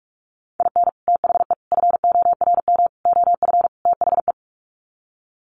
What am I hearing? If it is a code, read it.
Morse code, heard as a keyed tone, about 44 wpm, 700 Hz. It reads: IN THE FORM OF THE
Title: Sets of 5 Words Encore2 45wpm